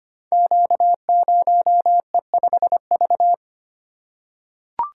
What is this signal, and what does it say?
Morse code, heard as a keyed tone, about 25 wpm, 700 Hz. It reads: Q0E5V
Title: Code Groups - 5 Letter:Numbers - 25wpm